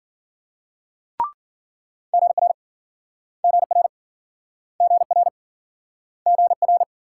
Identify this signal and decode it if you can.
Morse code, heard as a keyed tone, about 53 wpm, 700 Hz. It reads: GR GR GR GR